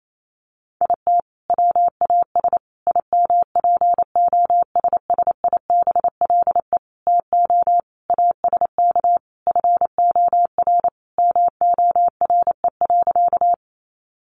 Morse code, heard as a keyed tone, about 28 wpm, 700 Hz. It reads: IT WAS IMPOSSIBLE TO ASK FOR MORE.
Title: Sentences from Top 1100 Words 28wpm